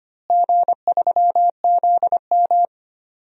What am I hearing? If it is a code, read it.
Morse code, heard as a keyed tone, about 25 wpm, 700 Hz. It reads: G3ZM
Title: DXCC-100 Call Signs - 4 Characters - Repeated 3x - 25wpm